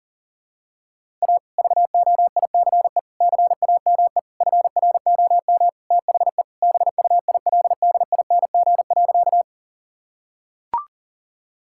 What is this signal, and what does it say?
Morse code, heard as a keyed tone, about 37 wpm, 700 Hz. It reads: A VOICE CAME FROM THE BUILDING.
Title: Sentences from Top 1000 Words - Review Part 5 - 40wpm